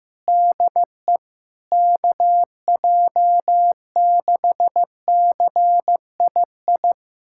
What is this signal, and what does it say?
Morse code, heard as a keyed tone, about 15 wpm, 700 Hz. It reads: DE KJ6CII
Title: USA Callsign DE Callsign - 6 Characters - 15wpm